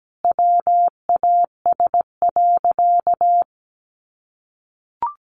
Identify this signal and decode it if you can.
Morse code, heard as a keyed tone, about 17 wpm, 700 Hz. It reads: WAS.
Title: Sentences from Top 900 Words 17wpm